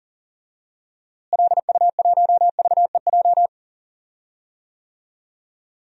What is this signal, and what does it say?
Morse code, heard as a keyed tone, about 38 wpm, 700 Hz. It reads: LU1VEJ
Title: DXCC-100 Call Signs - 6 Characters - Repeated 3x - 40wpm